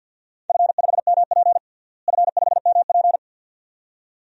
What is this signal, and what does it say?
Morse code, heard as a keyed tone, about 50 wpm, 700 Hz. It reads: F5MP F5MP
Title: DXCC-100 Call Signs - 4 Characters - Repeated 2x - 50wpm